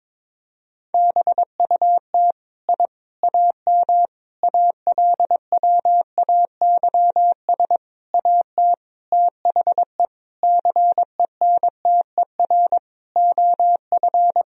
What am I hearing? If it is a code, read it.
Morse code, heard as a keyed tone, about 22 wpm, 700 Hz. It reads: BUT I AM ALWAYS AT THE CENTER OF
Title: Sentences from Top 1000 Words - Review Part 2 - 22wpm